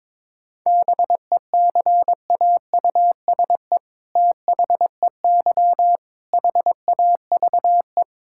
Morse code, heard as a keyed tone, about 22 wpm, 700 Hz. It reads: BECAUSE THEY HAVE
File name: All-Age Appropriate Jokes - 22wpm